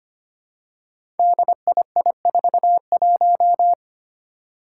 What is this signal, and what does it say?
Morse code, heard as a keyed tone, about 25 wpm, 700 Hz. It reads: DII41
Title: Five Challenging Letter-Number Combinations 25wpm